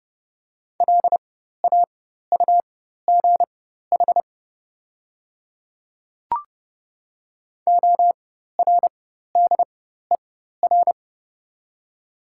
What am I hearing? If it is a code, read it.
Morse code, heard as a keyed tone, about 30 wpm, spaced as for 14 wpm, 700 Hz. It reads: LAUGH ORDER
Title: 4x Character Spacing - Top 500 Words - 30wpm